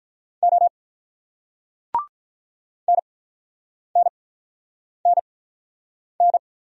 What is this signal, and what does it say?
Morse code, heard as a keyed tone, about 48 wpm, spaced as for 9 wpm, 700 Hz. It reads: O NNNN